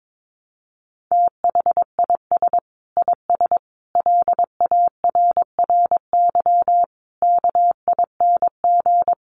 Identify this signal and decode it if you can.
Morse code, heard as a keyed tone, about 22 wpm, 700 Hz. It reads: THIS IS LARRY KING
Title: Sets of 5 Words Encore2 22wpm